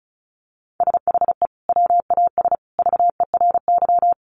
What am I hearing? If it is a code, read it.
Morse code, heard as a keyed tone, about 34 wpm, 700 Hz. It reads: SHE WAS VERY
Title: Sentences from Top 900 Words 35wpm